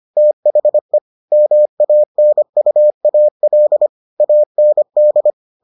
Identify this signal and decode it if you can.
Morse code, heard as a keyed tone, about 25 wpm, 600 Hz. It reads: THE MANUAL AND